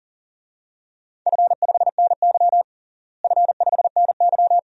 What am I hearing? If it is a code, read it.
Morse code, heard as a keyed tone, about 40 wpm, 700 Hz. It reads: F5NY F5NY
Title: DXCC-100 Call Signs - 4 Characters - Repeated 2x - 40wpm